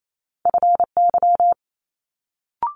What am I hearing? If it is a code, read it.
Morse code, heard as a keyed tone, about 28 wpm, 700 Hz. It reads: FY
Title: Multi-letter Phonemes Quick Fire 28wpm